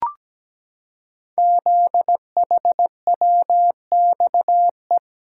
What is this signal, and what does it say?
Morse code, heard as a keyed tone, about 17 wpm, 700 Hz. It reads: ZHWXE
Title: Code Groups - 5 Letters - 17wpm